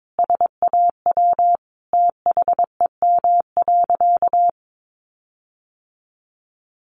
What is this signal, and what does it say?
Morse code, heard as a keyed tone, about 22 wpm, 700 Hz. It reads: SAW THEM.
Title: Sentences from Top 1300 Words 22wpm